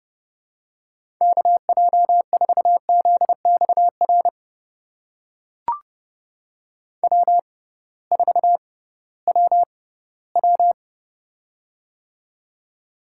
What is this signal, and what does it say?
Morse code, heard as a keyed tone, about 32 wpm, spaced as for 18 wpm, 700 Hz. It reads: KJ4ZXR W4WW
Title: USA Call Signs - 6x Character Spacing - 30wpm